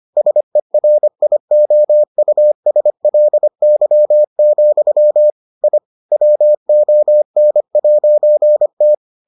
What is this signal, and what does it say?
Morse code, heard as a keyed tone, about 25 wpm, 600 Hz. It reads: SERIOUSLY, I WON'T